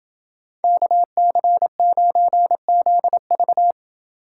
Morse code, heard as a keyed tone, about 30 wpm, 700 Hz. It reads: KC9ZV